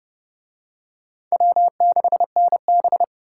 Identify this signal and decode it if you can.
Morse code, heard as a keyed tone, about 28 wpm, 700 Hz. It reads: W6NB